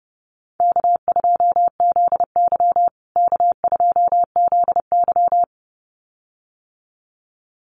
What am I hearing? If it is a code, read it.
Morse code, heard as a keyed tone, about 30 wpm, 700 Hz. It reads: K2ZY K2ZY
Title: USA Call Signs Repeated 2x 30wpm